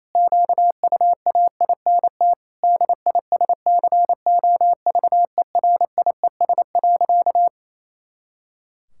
Morse code, heard as a keyed tone, about 28 wpm, 700 Hz. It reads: QUAINT DISCOVERIES.
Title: Pangram Challenge v2 - 28wpm